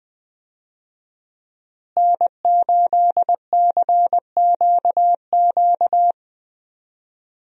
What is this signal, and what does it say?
Morse code, heard as a keyed tone, about 20 wpm, 700 Hz. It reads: N8CQQ
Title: USA Call Signs - 5 Characters - Repeated 3x - 20wpm